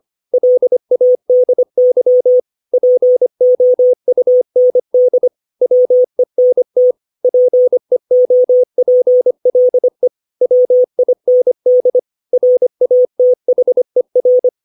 Morse code, heard as a keyed tone, about 25 wpm, 500 Hz. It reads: LADY POUND WENT PEOPLE WIND RATHER
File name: 632 words / 632 Words 25WPM